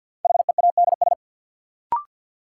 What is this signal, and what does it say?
Morse code, heard as a keyed tone, about 47 wpm, 700 Hz. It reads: HEADS
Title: Sentences from Top 1000 Words - Review Part 3 - New Word Drill - 50wpm